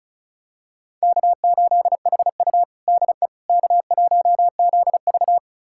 Morse code, heard as a keyed tone, about 36 wpm, 700 Hz. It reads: K8HU DE K1ZV
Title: USA Callsign DE Callsign - 4 Characters - 35wpm